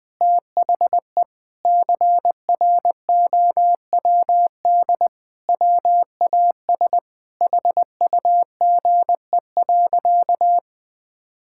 Morse code, heard as a keyed tone, about 20 wpm, 700 Hz. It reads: THE CROWD WAS HUGE.
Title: Sentences with 6-Word Phrases - 20wpm